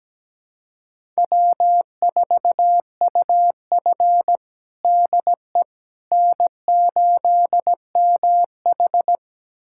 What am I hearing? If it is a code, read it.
Morse code, heard as a keyed tone, about 17 wpm, 700 Hz. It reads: W4UF DE N8MH
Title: USA Callsign DE Callsign - 4 Characters - 17wpm